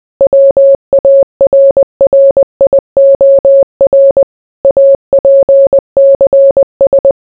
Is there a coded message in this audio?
Yes: Morse code from 0.2 to 7.1 s, heard as a keyed tone, about 20 wpm, 550 Hz. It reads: WARRIOR APCS